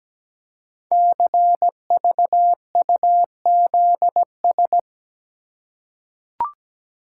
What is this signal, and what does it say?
Morse code, heard as a keyed tone, about 17 wpm, 700 Hz. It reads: CVUZS
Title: Code Groups - 5 Letters - 17wpm